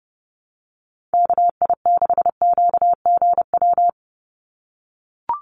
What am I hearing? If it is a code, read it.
Morse code, heard as a keyed tone, about 30 wpm, 700 Hz. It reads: KI6QGW
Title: USA Call Signs 30wpm